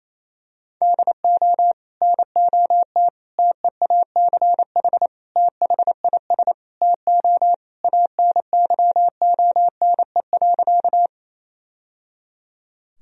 Morse code, heard as a keyed tone, about 28 wpm, 700 Hz. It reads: DO NOT TEACH THIS TO ANYONE.
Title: Sentences from Top 1000 Words - Review Part 2 - 28wpm